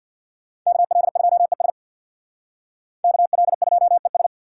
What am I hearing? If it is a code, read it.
Morse code, heard as a keyed tone, about 62 wpm, 700 Hz. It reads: KL2ES KL2ES
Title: Speed Racing - USA 2x2 Call Signs 50wpm